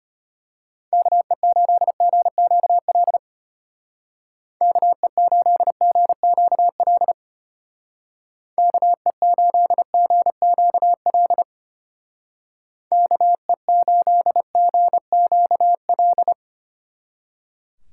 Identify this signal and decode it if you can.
Morse code, heard as a keyed tone, about 31 wpm, 700 Hz. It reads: KE8GQL KE8GQL KE8GQL KE8GQL
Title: Speed Racing - USA Call Signs 25wpm